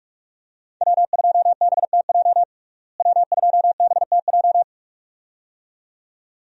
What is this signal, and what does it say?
Morse code, heard as a keyed tone, about 46 wpm, 700 Hz. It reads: W2BTJ W2BTJ
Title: USA Call Signs - 5 Characters - Repeated 2x - 45wpm